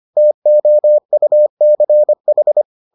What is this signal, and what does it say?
Morse code, heard as a keyed tone, about 25 wpm, 600 Hz. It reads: TOUCH